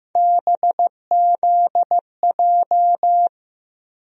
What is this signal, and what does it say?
Morse code, heard as a keyed tone, about 15 wpm, 700 Hz. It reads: BZJ
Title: USA Call Sign Postfixes - 3 Letters 15wpm